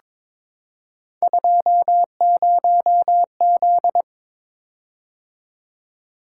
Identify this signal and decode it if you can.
Morse code, heard as a keyed tone, about 22 wpm, 700 Hz. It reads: 20Z
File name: practice / 22wpm / Three Letter-Number Combinations - 22wpm